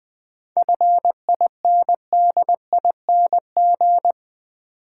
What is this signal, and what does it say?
Morse code, heard as a keyed tone, about 20 wpm, 700 Hz. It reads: FINDING
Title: Sentences from Top 1100 Words - New Word Drill Alt - 20wpm